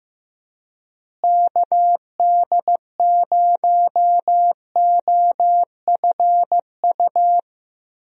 Morse code, heard as a keyed tone, about 15 wpm, 700 Hz. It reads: KD0OFU
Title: USA Call Signs - 6 Characters - Repeated 2x - 15wpm